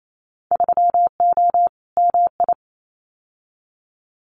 Morse code, heard as a keyed tone, about 28 wpm, 700 Hz. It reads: 3O MI
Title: Winter Field Day Exchanges 28wpm